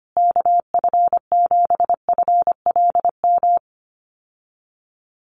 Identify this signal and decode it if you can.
Morse code, heard as a keyed tone, about 25 wpm, 700 Hz. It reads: KF7FLM
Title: USA Call Signs Repeated 3x 25wpm